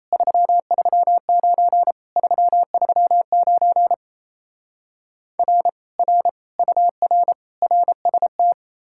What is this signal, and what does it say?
Morse code, heard as a keyed tone, about 37 wpm, 700 Hz. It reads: 339 339 R R UR RST